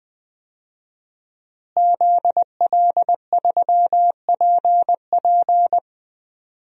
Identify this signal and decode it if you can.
Morse code, heard as a keyed tone, about 20 wpm, 700 Hz. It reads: ZL3PP
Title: DXCC-100 Call Signs - 5 Characters - Repeated 2x - 20wpm